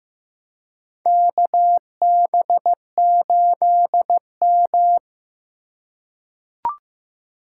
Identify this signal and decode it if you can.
Morse code, heard as a keyed tone, about 15 wpm, 700 Hz. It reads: KB8M
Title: K1USN SST QSO Practice - 15wpm